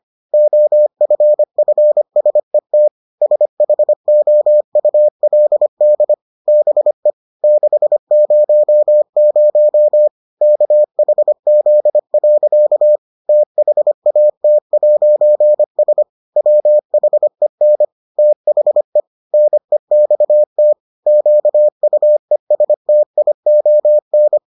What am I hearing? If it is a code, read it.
Morse code, heard as a keyed tone, about 25 wpm, 600 Hz. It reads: OFFSET SHOULD BE 600 KHZ. THAT'S WHEN THE NEXT QUESTION